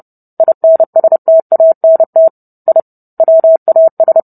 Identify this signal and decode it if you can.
Morse code, heard as a keyed tone, about 30 wpm, 650 Hz. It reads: INSTANT I WAS